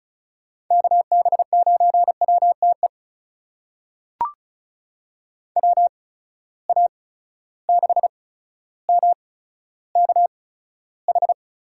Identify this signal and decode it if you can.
Morse code, heard as a keyed tone, about 36 wpm, spaced as for 15 wpm, 700 Hz. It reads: KB9WTE WA6MKH
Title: USA Call Signs - 6 Characters - 8x Character Spacing - 35wpm